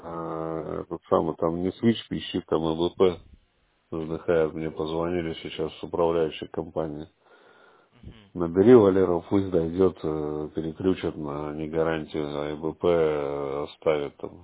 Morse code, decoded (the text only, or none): none